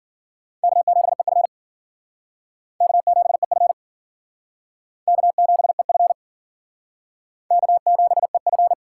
Morse code, heard as a keyed tone, about 50 wpm, 700 Hz. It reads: K7EF K7EF K7EF K7EF
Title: Speed Racing - USA 1x2 Call Signs 40wpm